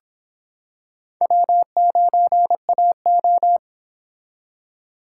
Morse code, heard as a keyed tone, about 26 wpm, 700 Hz. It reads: W9AO